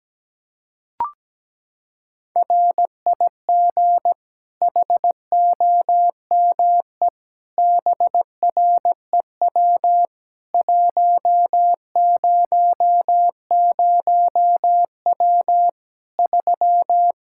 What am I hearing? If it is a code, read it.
Morse code, heard as a keyed tone, about 17 wpm, 700 Hz. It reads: RIG HOME BREW 100W 3
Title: K1USN SST QSO Practice 17wpm